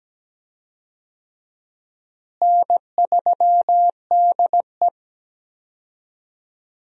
Morse code, heard as a keyed tone, about 17 wpm, 700 Hz. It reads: N3DE
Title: USA Call Signs - 4 Characters - Repeated 3x - 17wpm